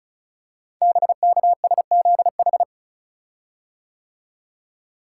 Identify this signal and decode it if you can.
Morse code, heard as a keyed tone, about 36 wpm, 700 Hz. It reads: BKSZH
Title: Code Groups - 5 Letters - 35wpm